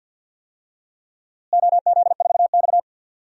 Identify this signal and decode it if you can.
Morse code, heard as a keyed tone, about 50 wpm, 700 Hz. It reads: OZ4X
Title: DXCC-100 Call Signs - 4 Characters - Repeated 3x - 50wpm